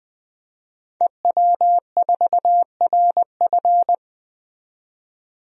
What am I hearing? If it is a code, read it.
Morse code, heard as a keyed tone, about 21 wpm, 700 Hz. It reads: EW4RF
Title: DXCC-100 Call Signs - 5 Characters - 20wpm